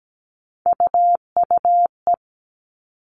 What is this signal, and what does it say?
Morse code, heard as a keyed tone, about 17 wpm, 700 Hz. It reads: UUE
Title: All Cut Numbers - 3-Digit 17wpm